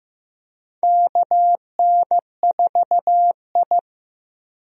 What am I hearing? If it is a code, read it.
Morse code, heard as a keyed tone, about 15 wpm, 700 Hz. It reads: KN4I
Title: USA Call Signs - 4 Characters - Repeated 3x - 15wpm